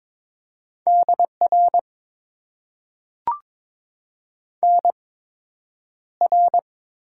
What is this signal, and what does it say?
Morse code, heard as a keyed tone, about 22 wpm, spaced as for 9 wpm, 700 Hz. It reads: DR NR
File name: Top 90 QSO Elements - 8x Character Spacing - 22wpm